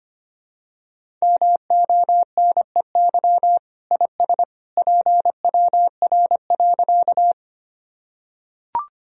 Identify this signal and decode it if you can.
Morse code, heard as a keyed tone, about 25 wpm, 700 Hz. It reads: MONEY IS PWR.
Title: Sentences from Top 500 Words and Abbreviations 25wpm